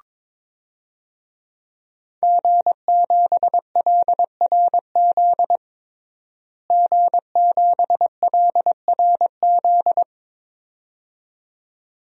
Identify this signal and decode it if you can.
Morse code, heard as a keyed tone, about 22 wpm, 700 Hz. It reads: G7LRZ G7LRZ
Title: DXCC-100 Call Signs - 5 Characters - Repeated 2x - 22wpm